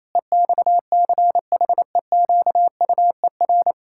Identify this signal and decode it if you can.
Morse code, heard as a keyed tone, about 28 wpm, 700 Hz. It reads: EXCHEQUER
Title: Pangram Challenge v2 - 28wpm